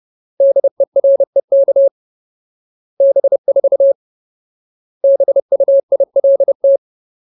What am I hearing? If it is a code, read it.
Morse code, heard as a keyed tone, about 30 wpm, 550 Hz. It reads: DEREK B4 BUILT
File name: Ultimate Word List / 30WPM Ultimate